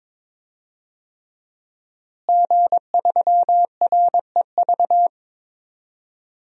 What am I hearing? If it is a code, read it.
Morse code, heard as a keyed tone, about 22 wpm, 700 Hz. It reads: G3REV
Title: DXCC-100 Call Signs - 5 Characters - Repeated 3x - 22wpm